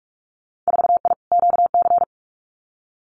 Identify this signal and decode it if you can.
Morse code, heard as a keyed tone, about 46 wpm, 700 Hz. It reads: 4I QC
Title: Winter Field Day Exchanges Encore 45wpm